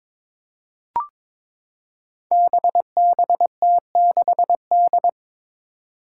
Morse code, heard as a keyed tone, about 22 wpm, 700 Hz. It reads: BBT6D
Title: Five Challenging Letter-Number Combinations 22wpm